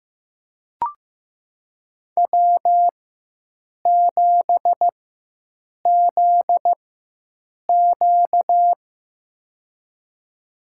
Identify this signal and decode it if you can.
Morse code, heard as a keyed tone, about 15 wpm, spaced as for 7 wpm, 700 Hz. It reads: W7ZQ
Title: USA Call Signs - 4x Character Spacing - 15wpm